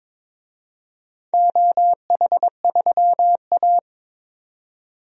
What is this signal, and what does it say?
Morse code, heard as a keyed tone, about 22 wpm, 700 Hz. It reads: OH3A
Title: DXCC-100 Call Signs - 4 Characters - 22wpm